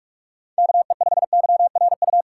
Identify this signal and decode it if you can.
Morse code, heard as a keyed tone, about 46 wpm, 700 Hz. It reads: KE5YRU
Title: USA Call Signs - 6 Characters - 6x Character Spacing - 45wpm